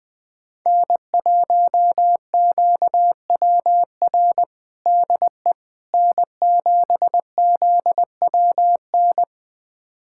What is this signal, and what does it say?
Morse code, heard as a keyed tone, about 21 wpm, 700 Hz. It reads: N1QWR DE N7ZWN
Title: USA Callsign DE Callsign - 5 Characters - 20wpm